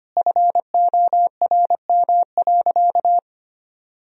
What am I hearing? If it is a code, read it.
Morse code, heard as a keyed tone, about 25 wpm, 700 Hz. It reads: FORM.
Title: Sentences from Top 1000 Words - Review Part 1 - 25wpm